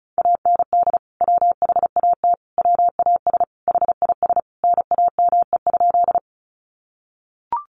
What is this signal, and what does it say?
Morse code, heard as a keyed tone, about 34 wpm, 700 Hz. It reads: AND WHAT WAS HIS NAME?
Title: Sentences from Top 200 Words 35wpm